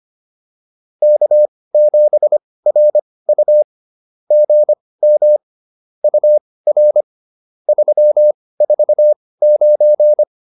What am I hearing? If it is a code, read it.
Morse code, heard as a keyed tone, about 25 wpm, spaced as for 18 wpm, 590 Hz. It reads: K7RU GM UR 349
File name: POTA206 18